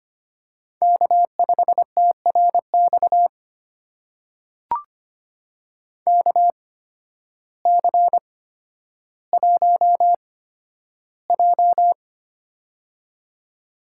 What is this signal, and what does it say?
Morse code, heard as a keyed tone, about 25 wpm, spaced as for 12 wpm, 700 Hz. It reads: K5TRX KC1J
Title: USA Call Signs - 8x Character Spacing - 25wpm